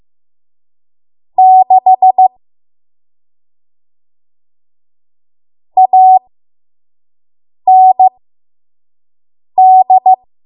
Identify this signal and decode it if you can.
Morse code, heard as a keyed tone, about 15 wpm, spaced as for 5 wpm, 750 Hz. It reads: 6 AND
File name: Morse / 05WPM